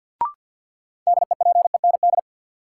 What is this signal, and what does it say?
Morse code, heard as a keyed tone, about 51 wpm, 700 Hz. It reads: DEPEND
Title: Sentences from Top 1300 Words - New Word Drill Alt - 50wpm